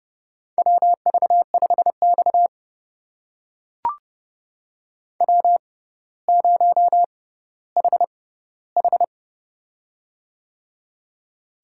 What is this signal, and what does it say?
Morse code, heard as a keyed tone, about 30 wpm, spaced as for 15 wpm, 700 Hz. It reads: WV5X W0HH
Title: USA Call Signs - 6x Character Spacing - 30wpm